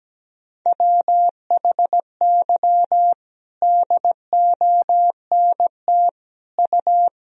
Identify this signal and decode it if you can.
Morse code, heard as a keyed tone, about 17 wpm, 700 Hz. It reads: WHY DONT U
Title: Sentences from Top 300 Words and Abbreviations 17wpm